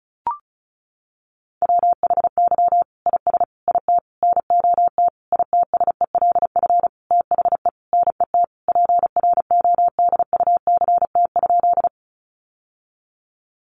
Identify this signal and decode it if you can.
Morse code, heard as a keyed tone, about 34 wpm, 700 Hz. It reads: WHY IS IT NOT ITSELF THE NET PRODUCT?
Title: Sentences from Top 1400 Words 35wpm